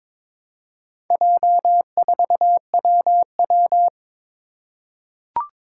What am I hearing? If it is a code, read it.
Morse code, heard as a keyed tone, about 22 wpm, 700 Hz. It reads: J4WW
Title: Four Challenging Letter-Number Combinations 22wpm